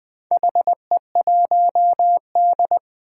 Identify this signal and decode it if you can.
Morse code, heard as a keyed tone, about 20 wpm, 700 Hz. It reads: HE1D
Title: Four Challenging Letter-Number Combinations 20wpm